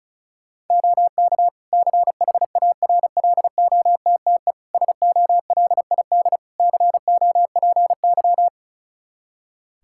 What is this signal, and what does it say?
Morse code, heard as a keyed tone, about 36 wpm, 700 Hz. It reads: OK CHARLOTTE SOLID COPY
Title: Practice QSO - Exchange 3 - More Info - 35wpm